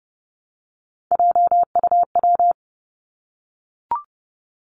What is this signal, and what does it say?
Morse code, heard as a keyed tone, about 32 wpm, 700 Hz. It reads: JUW
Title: Three Challenging Letter-Number Combinations 30wpm